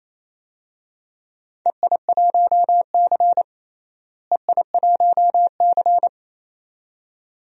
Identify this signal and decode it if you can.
Morse code, heard as a keyed tone, about 28 wpm, 700 Hz. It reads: EI1C EI1C
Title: DXCC-100 Call Signs - 4 Characters - Repeated 2x - 28wpm